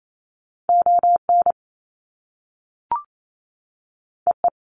Morse code, heard as a keyed tone, about 28 wpm, 700 Hz. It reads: ON EE
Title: Top Two Letter Combinations in Words 28wpm